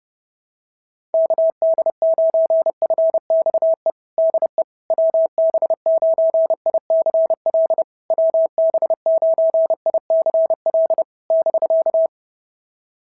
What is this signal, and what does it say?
Morse code, heard as a keyed tone, about 30 wpm, 660 Hz. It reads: KD9FXE DE WB9ICL WB9ICL <BK>
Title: SOTA QSO Practice 30wpm